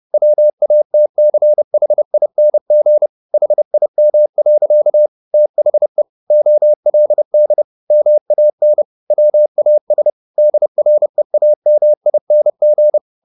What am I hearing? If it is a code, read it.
Morse code, heard as a keyed tone, about 30 wpm, 600 Hz. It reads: WATCHING HIM. THE OLD MAN WAS DREAMING